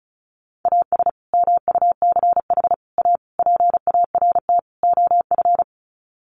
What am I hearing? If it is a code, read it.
Morse code, heard as a keyed tone, about 34 wpm, 700 Hz. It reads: AS MUCH A PART OF
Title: Sets of 5 Words Encore2 35wpm